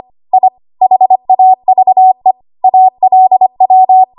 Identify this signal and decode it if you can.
Morse code, heard as a keyed tone, about 25 wpm, 750 Hz. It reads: I HAVE ALW
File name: Archive / 25 WPM / 25WPM